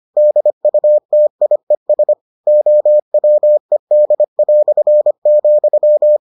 Morse code, heard as a keyed tone, about 25 wpm, 600 Hz. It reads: DUTIES OWED",